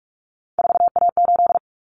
Morse code, heard as a keyed tone, about 44 wpm, 700 Hz. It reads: 4A8
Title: Common Cut Numbers - 3-Digit 45wpm